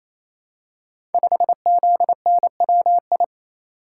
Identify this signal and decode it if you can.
Morse code, heard as a keyed tone, about 29 wpm, 700 Hz. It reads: 5ZNWI
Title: Code Groups - 5 Letter:Numbers - 28wpm